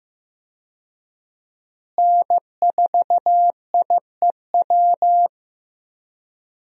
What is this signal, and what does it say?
Morse code, heard as a keyed tone, about 15 wpm, 700 Hz. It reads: N4IEW
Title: USA Call Signs - 5 Characters - Repeated 2x - 15wpm